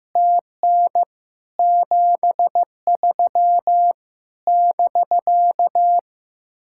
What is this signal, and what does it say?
Morse code, heard as a keyed tone, about 15 wpm, 700 Hz. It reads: TN 73 <BK>
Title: Speed-Racing Common QSO Phrases Encore - 15wpm